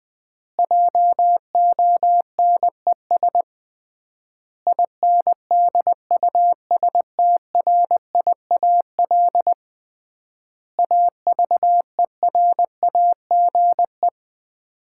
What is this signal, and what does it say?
Morse code, heard as a keyed tone, about 20 wpm, 700 Hz. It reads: JONES INDUSTRIAL AVERAGE
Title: 3x Word-Spacing - Sets of 5 Words - 20wpm